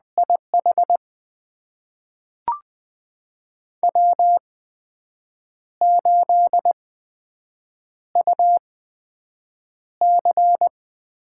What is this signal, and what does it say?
Morse code, heard as a keyed tone, about 20 wpm, spaced as for 7 wpm, 700 Hz. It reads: IH W8UC
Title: USA Call Signs - 8x Character Spacing - 20wpm